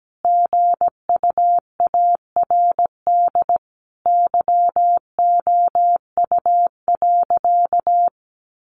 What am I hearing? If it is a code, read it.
Morse code, heard as a keyed tone, about 17 wpm, 700 Hz. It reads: GUARD YOU.
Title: Sentences from Top 700 Words 17wpm